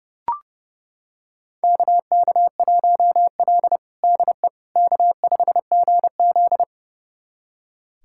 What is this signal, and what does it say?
Morse code, heard as a keyed tone, about 32 wpm, 700 Hz. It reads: KK1L DE K5GZ
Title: USA Callsign DE Callsign - 4 Characters - 30wpm